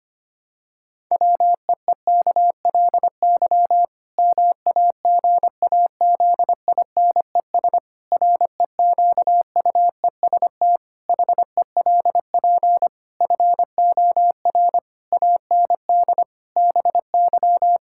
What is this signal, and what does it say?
Morse code, heard as a keyed tone, about 25 wpm, 700 Hz. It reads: WEEKLY MAGAZINES REQUEST HELP FOR AND BY
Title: Pangram Challenge v2 - 25wpm